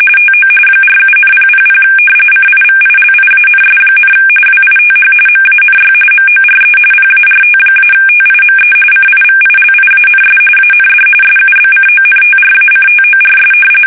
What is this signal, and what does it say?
RTTY, about 45.45 baud. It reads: TO 47,000 CIVILIANS, WITH MANY MORE PART-TIME CIVILIANS AND SUBCONTRACTORS
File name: fsk850 / battleshipnj